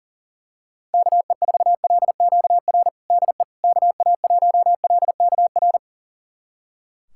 Morse code, heard as a keyed tone, about 40 wpm, 700 Hz. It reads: KE4LQR DE KA1LKR
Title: USA Callsign DE Callsign - 6 Characters - 40wpm